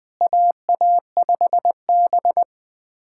Morse code, heard as a keyed tone, about 20 wpm, 700 Hz. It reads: AA5B
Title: K1USN SST QSO Practice - 20wpm